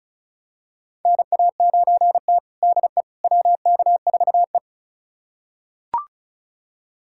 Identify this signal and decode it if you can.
Morse code, heard as a keyed tone, about 34 wpm, 700 Hz. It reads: NA9T DE WK4E
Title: USA Callsign DE Callsign - 4 Characters - 35wpm